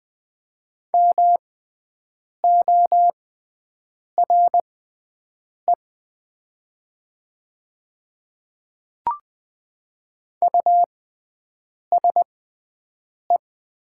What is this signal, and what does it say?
Morse code, heard as a keyed tone, about 20 wpm, spaced as for 7 wpm, 700 Hz. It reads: MORE USE
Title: 6x Character Spacing - Top 200 Words - 20wpm